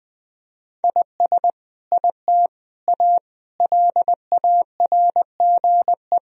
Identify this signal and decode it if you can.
Morse code, heard as a keyed tone, about 20 wpm, 700 Hz. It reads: IS IT A LARGE
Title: Sentences from Top 1000 Words - Review Part 4 - 20wpm